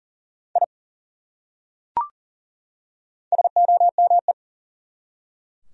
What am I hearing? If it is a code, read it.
Morse code, heard as a keyed tone, about 38 wpm, 700 Hz. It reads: I SOME
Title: Top 100 Words 40wpm